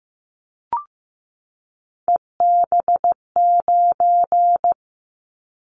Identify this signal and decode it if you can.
Morse code, heard as a keyed tone, about 15 wpm, 700 Hz. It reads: EB9
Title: Callsign Prefixes for Top 100 DXCC - 15wpm